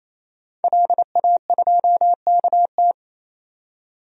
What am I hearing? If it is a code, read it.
Morse code, heard as a keyed tone, about 28 wpm, 700 Hz. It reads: LA2KT
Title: DXCC-100 Call Signs - 5 Characters - 28wpm